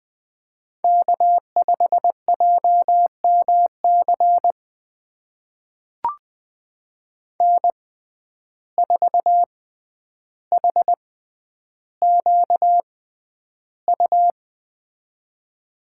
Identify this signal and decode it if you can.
Morse code, heard as a keyed tone, about 20 wpm, spaced as for 10 wpm, 700 Hz. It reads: K5JMC N4HQU
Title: USA Call Signs - 5 Characters - 6x Character Spacing - 20wpm